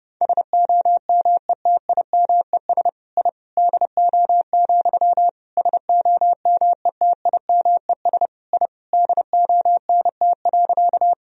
Morse code, heard as a keyed tone, about 30 wpm, 700 Hz. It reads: SOMETIMES I DO, SOMETIMES I DONT.
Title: Sentences from Top 1000 Words - Review Part 3 - 30wpm